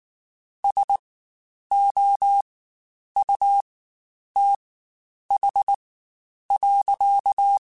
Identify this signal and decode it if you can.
Morse code, heard as a keyed tone, about 19 wpm, spaced as for 9 wpm, 800 Hz. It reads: SOUTH.
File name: Spaced Characters / Spaced Characters-mp3s / Weather 4 21 sp @18